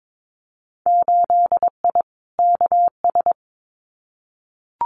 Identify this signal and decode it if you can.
Morse code, heard as a keyed tone, about 22 wpm, 700 Hz. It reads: 8I KS
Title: Winter Field Day Exchanges Encore 22wpm